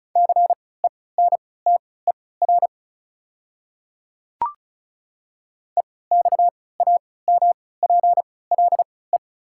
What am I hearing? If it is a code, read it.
Morse code, heard as a keyed tone, about 36 wpm, spaced as for 20 wpm, 690 Hz. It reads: CENTER EXAMPLE
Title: 3x Character Spacing - Top 500 Words - 35wpm